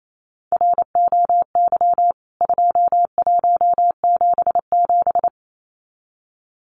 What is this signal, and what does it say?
Morse code, heard as a keyed tone, about 28 wpm, 700 Hz. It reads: ROY 2177
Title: CWT Exchange 28wpm